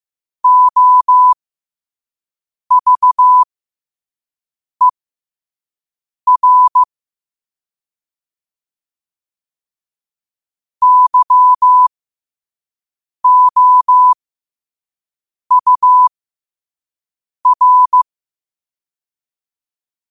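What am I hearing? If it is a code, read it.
Morse code, heard as a keyed tone, about 15 wpm, spaced as for 5 wpm, 1000 Hz. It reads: OVER YOUR
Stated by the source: Morse code quotes sent at 5 words per minute.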